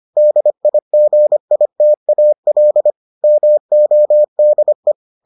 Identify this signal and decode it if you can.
Morse code, heard as a keyed tone, about 25 wpm, 600 Hz. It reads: DIGITAL MODE